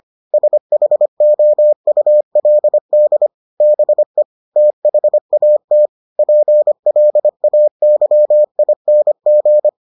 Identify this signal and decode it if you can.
Morse code, heard as a keyed tone, about 25 wpm, 600 Hz. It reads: SHOULD BE THAT PLAYING